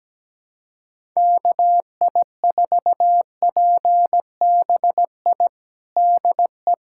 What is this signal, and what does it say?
Morse code, heard as a keyed tone, about 17 wpm, 700 Hz. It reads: KI4PBI DE
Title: USA Callsign DE Callsign 17wpm